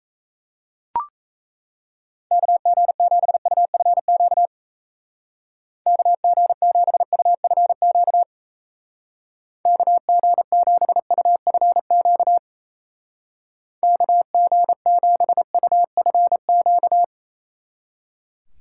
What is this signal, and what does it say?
Morse code, heard as a keyed tone, about 35 wpm, 700 Hz. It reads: KG7UFQ KG7UFQ KG7UFQ KG7UFQ